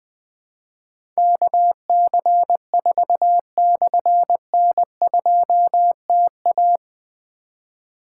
Morse code, heard as a keyed tone, about 20 wpm, 700 Hz. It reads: KC4/N2TA
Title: Difficult Call Signs 20wpm